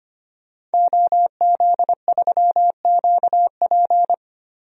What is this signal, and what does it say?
Morse code, heard as a keyed tone, about 25 wpm, 700 Hz. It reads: OZ3QP
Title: DXCC-100 Call Signs - 5 Characters - Repeated 3x - 25wpm